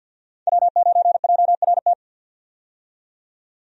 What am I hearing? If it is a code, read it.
Morse code, heard as a keyed tone, about 51 wpm, 700 Hz. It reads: W9JRT
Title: USA Call Signs - 5 Characters - 50wpm